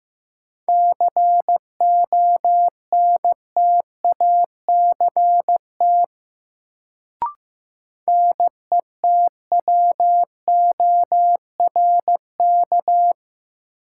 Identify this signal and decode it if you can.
Morse code, heard as a keyed tone, about 16 wpm, 700 Hz. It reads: CONTACT NETWORK
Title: Sentences from Top 1000 Words - Review Part 7 - New Word Drill - 15wpm